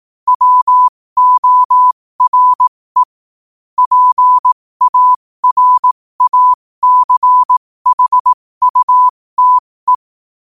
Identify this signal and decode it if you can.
Morse code, heard as a keyed tone, about 18 wpm, spaced as for 15 wpm, 1000 Hz. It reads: WORE PARACHUTE
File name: qotdfiles / 15wpm